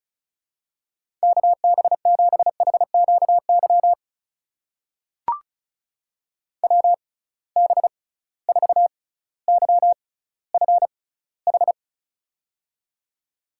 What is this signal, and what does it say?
Morse code, heard as a keyed tone, about 36 wpm, spaced as for 18 wpm, 700 Hz. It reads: KB7HQY WB4YFH
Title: USA Call Signs - 6 Characters - 6x Character Spacing - 35wpm